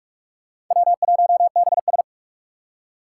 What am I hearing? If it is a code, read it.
Morse code, heard as a keyed tone, about 46 wpm, 700 Hz. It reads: W1BS
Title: Four Challenging Letter-Number Combinations 45wpm